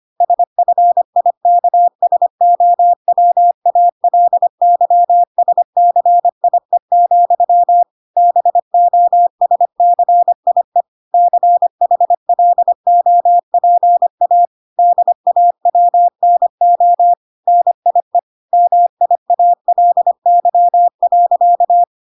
Morse code, heard as a keyed tone, about 25 wpm, 700 Hz. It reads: SFIKSOWALYSCIE, BO SCIE CHLOPA DAWNO NIE MIALY.